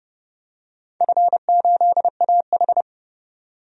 Morse code, heard as a keyed tone, about 30 wpm, 700 Hz. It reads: F8AH
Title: DXCC-100 Call Signs - 4 Characters - 30wpm